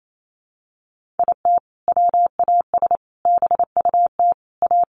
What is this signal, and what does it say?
Morse code, heard as a keyed tone, about 28 wpm, 700 Hz. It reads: IT WAS BUT A
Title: Sentences from Top 1500 Words 28wpm